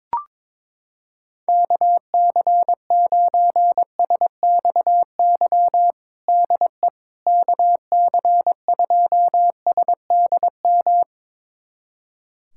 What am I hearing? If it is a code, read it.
Morse code, heard as a keyed tone, about 22 wpm, 700 Hz. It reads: KC9SXY DE KC2SDM
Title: USA Callsign DE Callsign - 6 Characters - 22wpm